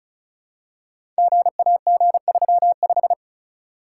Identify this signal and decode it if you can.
Morse code, heard as a keyed tone, about 36 wpm, 700 Hz. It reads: GAG35
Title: Code Groups - 5 Letter:Numbers - 35wpm